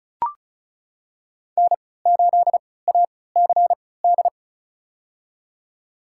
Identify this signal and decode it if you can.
Morse code, heard as a keyed tone, about 36 wpm, spaced as for 20 wpm, 700 Hz. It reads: N8ACD
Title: USA Call Signs - 3x Character Spacing - 35wpm